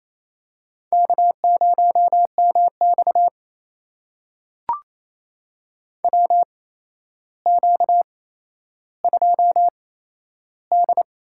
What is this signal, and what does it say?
Morse code, heard as a keyed tone, about 28 wpm, spaced as for 12 wpm, 700 Hz. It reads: K0MX WQ2D
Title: USA Call Signs - 4 Characters - 8x Character Spacing - 28wpm